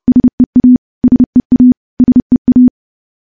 Morse code, heard as a keyed tone, about 31 wpm, 250 Hz. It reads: SEA SEA SEA
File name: morse / morse-training-30WPM